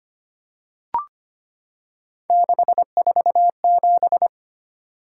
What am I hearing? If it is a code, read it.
Morse code, heard as a keyed tone, about 25 wpm, 700 Hz. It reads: 647
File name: Three Letter-Number Combinations - 25wpm